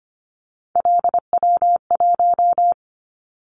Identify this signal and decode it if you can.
Morse code, heard as a keyed tone, about 25 wpm, 700 Hz. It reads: LW1
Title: Callsign Prefixes for Top 100 DXCC - 25wpm